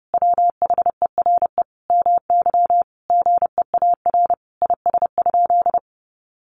Morse code, heard as a keyed tone, about 28 wpm, 700 Hz. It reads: WHERE MY GEAR IS?
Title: Sentences from Top 1400 Words 30wpm